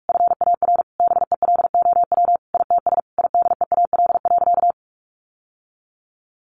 Morse code, heard as a keyed tone, about 44 wpm, 700 Hz. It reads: FAR BELOW ITS IDEAL.
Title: Sentences from Top 1400 Words 45wpm